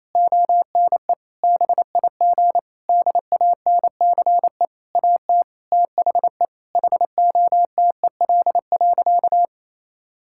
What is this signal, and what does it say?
Morse code, heard as a keyed tone, about 28 wpm, 700 Hz. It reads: ONE BIG DANCE AT THE HOTEL.
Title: Sentences from Top 1600 Words 28wpm